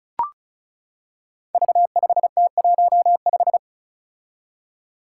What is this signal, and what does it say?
Morse code, heard as a keyed tone, about 36 wpm, 700 Hz. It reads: V5T15
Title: Five Challenging Letter-Number Combinations 35wpm